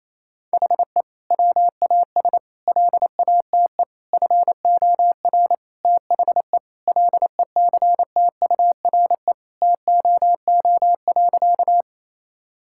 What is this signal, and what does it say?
Morse code, heard as a keyed tone, about 28 wpm, 700 Hz. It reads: HE WAS LATE FOR THE LECTURE TOO.
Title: Sentences from Top 1600 Words 28wpm